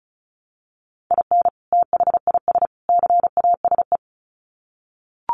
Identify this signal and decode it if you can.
Morse code, heard as a keyed tone, about 34 wpm, 700 Hz. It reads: IN THIS CASE
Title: Sets of 3 Words 35wpm